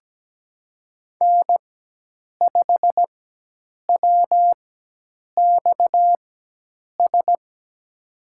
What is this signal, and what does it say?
Morse code, heard as a keyed tone, about 17 wpm, spaced as for 8 wpm, 700 Hz. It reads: N5WXS
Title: USA Call Signs - 5 Characters - 4x Character Spacing - 17wpm